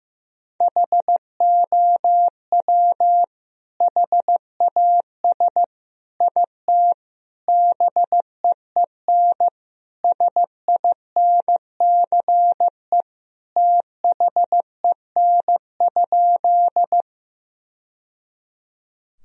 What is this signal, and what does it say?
Morse code, heard as a keyed tone, about 15 wpm, 700 Hz. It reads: HOW HAS IT BEEN SINCE THEN?
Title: Sentences from Top 1000 Words - Review Part 1 - 15wpm